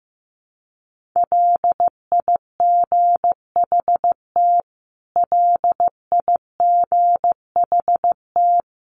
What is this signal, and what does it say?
Morse code, heard as a keyed tone, about 15 wpm, 700 Hz. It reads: LIGHT LIGHT
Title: Top 500 Words Repeated 15wpm